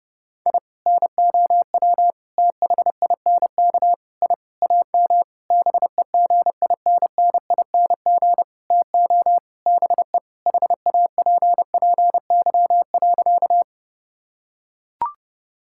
Morse code, heard as a keyed tone, about 30 wpm, 700 Hz. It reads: I NOW THINK I AM BEGINNING TO BE HAPPY.
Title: Sentences from Top 1000 Words - Review Part 5 - 30wpm